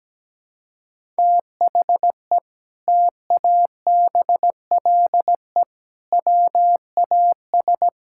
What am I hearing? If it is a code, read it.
Morse code, heard as a keyed tone, about 17 wpm, 700 Hz. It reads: THE TABLE WAS
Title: Sentences from Top 1000 Words - Review Part 6 - 17wpm